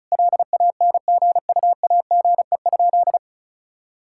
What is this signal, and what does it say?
Morse code, heard as a keyed tone, about 36 wpm, 700 Hz. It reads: LANGUAGE?
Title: Sentences from Top 1000 Words - Review Part 5 - 35wpm